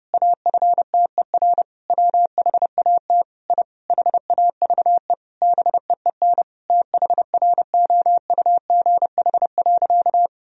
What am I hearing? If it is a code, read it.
Morse code, heard as a keyed tone, about 32 wpm, 700 Hz. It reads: AFTER WHAT I HAVE BEEN THROUGH.
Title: Sentences from Top 1000 Words - Review Part 1 - 30wpm